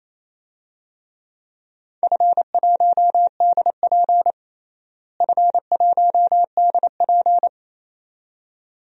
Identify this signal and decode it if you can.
Morse code, heard as a keyed tone, about 28 wpm, 700 Hz. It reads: F1DP F1DP
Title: DXCC-100 Call Signs - 4 Characters - Repeated 2x - 28wpm